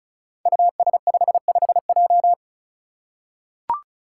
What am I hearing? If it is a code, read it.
Morse code, heard as a keyed tone, about 36 wpm, 700 Hz. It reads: US55J
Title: Five Challenging Letter-Number Combinations 35wpm